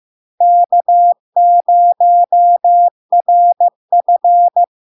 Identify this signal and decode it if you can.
Morse code, heard as a keyed tone, about 15 wpm, 700 Hz. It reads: K0RF
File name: LARK-FD-2022-Log-in-CW-15wpm-700-Hz